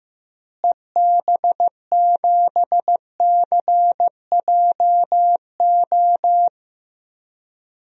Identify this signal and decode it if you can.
Morse code, heard as a keyed tone, about 16 wpm, 700 Hz. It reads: EB7CJO
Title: DXCC-100 Call Signs - 6 Characters - Repeated 3x - 15wpm